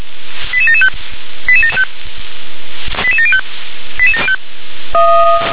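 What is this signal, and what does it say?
Five-tone selective call (ZVEI), heard as quick runs of five tones, tones of about 70 ms. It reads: 88775 88705 88705
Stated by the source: ZVEI five-tone selective call received on SDR.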